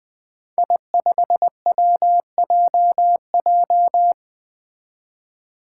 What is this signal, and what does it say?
Morse code, heard as a keyed tone, about 21 wpm, 700 Hz. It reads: I5WJJ
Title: Five Challenging Letter-Number Combinations 20wpm